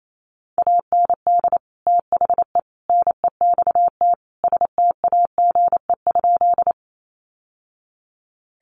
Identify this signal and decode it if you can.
Morse code, heard as a keyed tone, about 28 wpm, 700 Hz. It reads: AND THE NEXT STAGE?
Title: Sentences from Top 1200 Words 28wpm